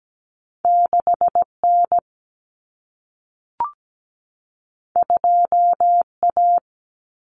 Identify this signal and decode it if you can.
Morse code, heard as a keyed tone, about 17 wpm, 700 Hz. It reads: 6N 2A
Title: Common Cut Numbers - 2-Digit 17wpm